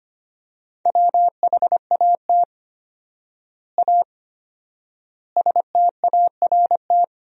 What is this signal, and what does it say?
Morse code, heard as a keyed tone, about 25 wpm, 700 Hz. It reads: WHAT A START
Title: 4x Word-Spacing - Sentences from Top 300 Words - 25wpm